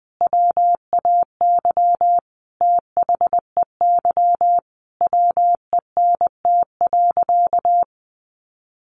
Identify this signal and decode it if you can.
Morse code, heard as a keyed tone, about 19 wpm, 700 Hz. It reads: WAY THEY WENT.
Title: Sentences from Top 1300 Words 20wpm